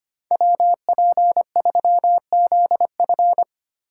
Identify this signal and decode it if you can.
Morse code, heard as a keyed tone, about 25 wpm, 700 Hz. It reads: WP3ZF
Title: USA Call Signs - 5 Characters - 4x Character Spacing - 25wpm